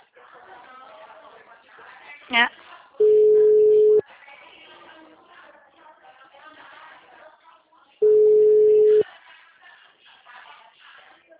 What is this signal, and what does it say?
Ringback tone (Europe, 425 Hz), with speech.